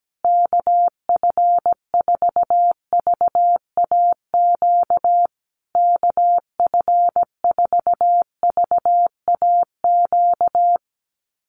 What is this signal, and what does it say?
Morse code, heard as a keyed tone, about 17 wpm, 700 Hz. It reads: KF4VAQ KF4VAQ
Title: USA Call Signs Repeated 3x 17wpm